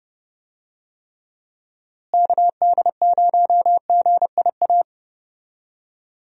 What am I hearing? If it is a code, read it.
Morse code, heard as a keyed tone, about 30 wpm, 700 Hz. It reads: KD0GIA
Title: USA Call Signs - 6 Characters - Repeated 3x - 30wpm